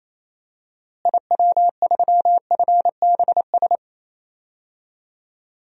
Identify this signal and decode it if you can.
Morse code, heard as a keyed tone, about 28 wpm, 700 Hz. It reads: IW3FBS
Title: DXCC-100 Call Signs - 6 Characters - Repeated 2x - 28wpm